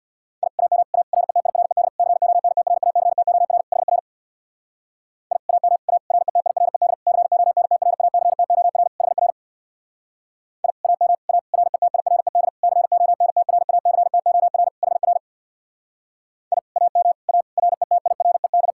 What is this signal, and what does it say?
Morse code, heard as a keyed tone, about 66 wpm, 700 Hz. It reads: I AM A RETIRED CONTRACTOR HR I AM A RETIRED CONTRACTOR HR I AM A RETIRED CONTRACTOR HR I AM A RETIRED